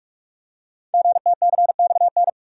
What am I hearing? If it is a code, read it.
Morse code, heard as a keyed tone, about 44 wpm, 700 Hz. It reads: GTCXN
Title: Code Groups - 5 Letters - 45wpm